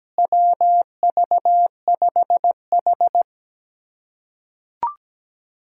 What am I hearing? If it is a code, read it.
Morse code, heard as a keyed tone, about 17 wpm, 700 Hz. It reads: WV5H
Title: USA Callsign DE Callsign - 4 Characters - 17wpm